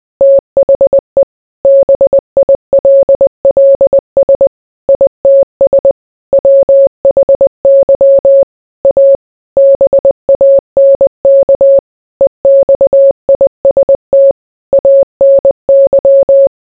Morse code, heard as a keyed tone, about 20 wpm, 550 Hz. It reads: THE BILLS ITS WHY A BANK EXIST ANY